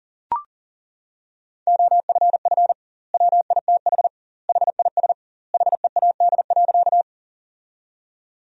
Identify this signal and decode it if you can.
Morse code, heard as a keyed tone, about 40 wpm, 700 Hz. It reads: OFF WITH HIS HEAD.
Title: 2x Word-Spacing - Sentences from Top 300 Words - 40wpm